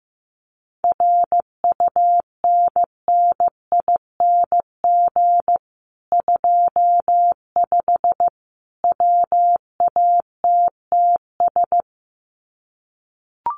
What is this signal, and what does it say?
Morse code, heard as a keyed tone, about 15 wpm, 700 Hz. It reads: RUNNING 25 WATTS
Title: Common QSO Phrases 15wpm